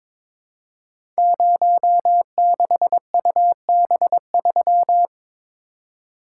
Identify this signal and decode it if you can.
Morse code, heard as a keyed tone, about 22 wpm, 700 Hz. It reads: 06UB3
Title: Code Groups - 5 Letter:Numbers - 22wpm